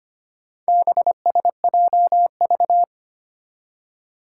Morse code, heard as a keyed tone, about 25 wpm, 700 Hz. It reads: BSJV
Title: Four Challenging Letter-Number Combinations 25wpm